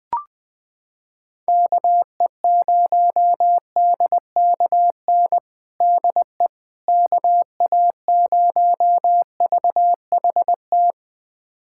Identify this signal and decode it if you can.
Morse code, heard as a keyed tone, about 20 wpm, 700 Hz. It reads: KE0DKN DE KA0VHT
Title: USA Callsign DE Callsign - 6 Characters - 20wpm